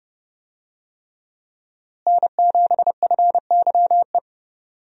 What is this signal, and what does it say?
Morse code, heard as a keyed tone, about 31 wpm, 700 Hz. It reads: N7FYE
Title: USA Call Signs - 5 Characters - Repeated 3x - 30wpm